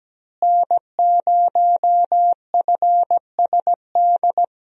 Morse code, heard as a keyed tone, about 17 wpm, 700 Hz. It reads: N0FSD
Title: USA Call Signs - 5 Characters - Repeated 3x - 17wpm